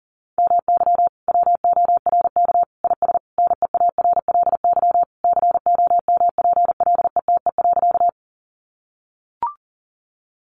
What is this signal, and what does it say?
Morse code, heard as a keyed tone, about 37 wpm, 700 Hz. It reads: MY WORK IS NEARLY COMPLETE.
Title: Sentences from Top 1300 Words 40wpm